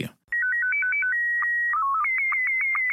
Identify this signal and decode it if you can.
SSTV (slow-scan television), Scottie 1.